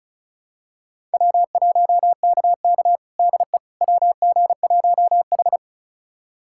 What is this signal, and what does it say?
Morse code, heard as a keyed tone, about 36 wpm, 700 Hz. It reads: W1KK DE WG1H
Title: USA Callsign DE Callsign - 4 Characters - 35wpm